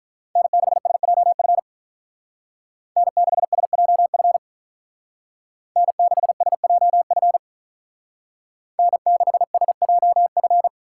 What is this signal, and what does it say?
Morse code, heard as a keyed tone, about 43 wpm, 700 Hz. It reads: N6SJF N6SJF N6SJF N6SJF
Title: Speed Racing - USA 1x3 Call Signs 35wpm